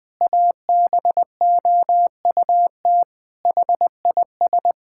Morse code, heard as a keyed tone, about 20 wpm, 700 Hz. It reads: ABOUT HIS
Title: Pangram Challenge v2 - 20wpm